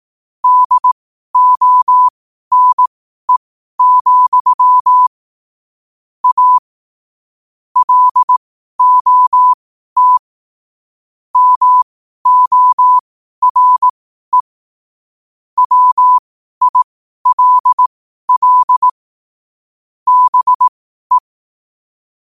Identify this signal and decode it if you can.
Morse code, heard as a keyed tone, about 18 wpm, spaced as for 12 wpm, 1000 Hz. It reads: DONE, A LOT MORE WILL BE
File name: qotdfiles / 12wpm